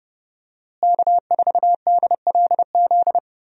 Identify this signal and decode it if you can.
Morse code, heard as a keyed tone, about 31 wpm, 700 Hz. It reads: K4DLZ
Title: USA Call Signs - 5 Characters - Repeated 3x - 30wpm